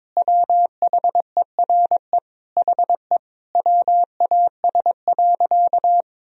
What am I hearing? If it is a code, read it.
Morse code, heard as a keyed tone, about 22 wpm, 700 Hz. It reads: WHERE HE WAS.
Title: Sentences from Top 1000 Words - Review Part 1 - 22wpm